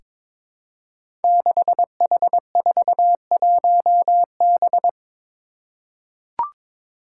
Morse code, heard as a keyed tone, about 22 wpm, 700 Hz. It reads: 6H41B
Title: Five Challenging Letter-Number Combinations 22wpm